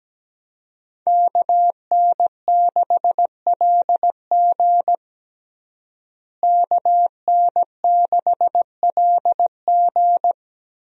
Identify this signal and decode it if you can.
Morse code, heard as a keyed tone, about 17 wpm, 700 Hz. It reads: KN6LG KN6LG
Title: USA Call Signs - 5 Characters - Repeated 2x - 17wpm